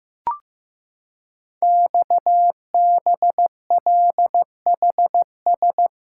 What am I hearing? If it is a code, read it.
Morse code, heard as a keyed tone, about 15 wpm, 700 Hz. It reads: XBLHS
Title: Code Groups - 5 Letters - 15wpm